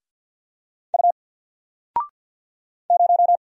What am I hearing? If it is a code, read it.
Morse code, heard as a keyed tone, about 50 wpm, 700 Hz. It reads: U 0
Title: Single Letter-Number 50wpm